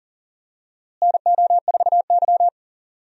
Morse code, heard as a keyed tone, about 40 wpm, 700 Hz. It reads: NO4Y
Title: USA Call Signs - 4 Characters - Repeated 2x - 40wpm